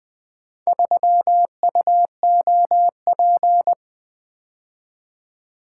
Morse code, heard as a keyed tone, about 25 wpm, 700 Hz. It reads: 3UOP
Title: Speed-Racing - Four Letter-Number Combinations - 20wpm